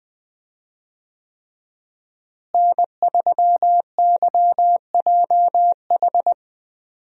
Morse code, heard as a keyed tone, about 20 wpm, 700 Hz. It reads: N3YJH
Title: USA Call Signs - 5 Characters - Repeated 3x - 20wpm